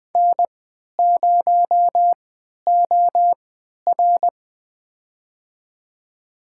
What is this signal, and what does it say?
Morse code, heard as a keyed tone, about 21 wpm, spaced as for 11 wpm, 700 Hz. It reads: N0OR
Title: USA Call Signs - 4 Characters - 3x Character Spacing - 20wpm